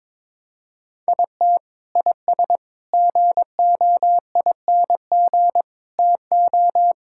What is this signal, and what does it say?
Morse code, heard as a keyed tone, about 22 wpm, 700 Hz. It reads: IT IS GOING TO
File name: practice / 22wpm / Sentences with 4-Word Phrases - 22wpm